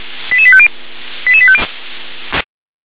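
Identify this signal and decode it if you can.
Five-tone selective call (ZVEI), heard as quick runs of five tones, tones of about 70 ms. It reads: 88850 88850
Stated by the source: ZVEI five-tone selective call received on SDR.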